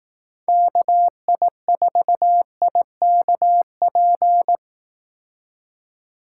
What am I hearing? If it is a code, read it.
Morse code, heard as a keyed tone, about 18 wpm, 700 Hz. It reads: KI4IKP
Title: Speed Racing - USA 2x3 Call Signs 15wpm